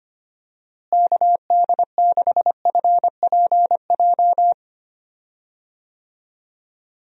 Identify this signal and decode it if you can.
Morse code, heard as a keyed tone, about 25 wpm, 700 Hz. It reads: KD6FPJ
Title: USA Call Signs - 6 Characters - Repeated 3x - 25wpm